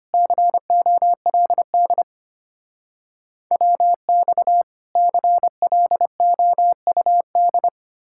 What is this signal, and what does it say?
Morse code, heard as a keyed tone, about 33 wpm, 700 Hz. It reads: COLD WX CLOUD